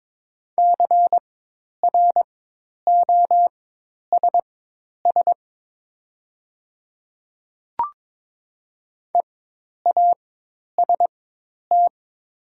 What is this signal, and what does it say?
Morse code, heard as a keyed tone, about 22 wpm, spaced as for 10 wpm, 700 Hz. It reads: CROSS EAST
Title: 4x Character Spacing - Top 500 Words - 22wpm